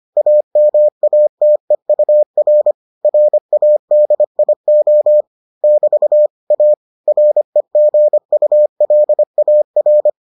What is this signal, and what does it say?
Morse code, heard as a keyed tone, about 25 wpm, 600 Hz. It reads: AMATEUR RADIO <BT> A REGULAR